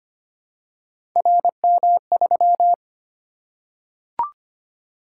Morse code, heard as a keyed tone, about 25 wpm, 700 Hz. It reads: RM3
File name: Three Letter-Number Combinations - 25wpm